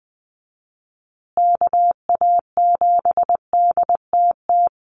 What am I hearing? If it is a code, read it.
Morse code, heard as a keyed tone, about 20 wpm, 700 Hz. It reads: KA7DTT
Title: USA Call Signs Repeated 2x 20wpm